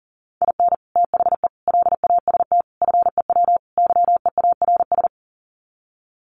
Morse code, heard as a keyed tone, about 37 wpm, 700 Hz. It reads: IN THE LAST FEW YEARS
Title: Sets of 5 Words 40wpm